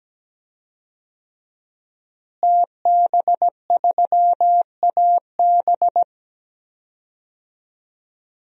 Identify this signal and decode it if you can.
Morse code, heard as a keyed tone, about 17 wpm, 700 Hz. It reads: TB3AB
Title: DXCC-100 Call Signs - 5 Characters - Repeated 3x - 17wpm